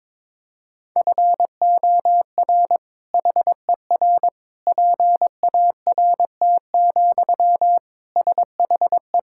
Morse code, heard as a keyed tone, about 22 wpm, 700 Hz. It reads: FOR HER PART, SHE
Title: Sentences from Top 1000 Words - Review Part 5 - 22wpm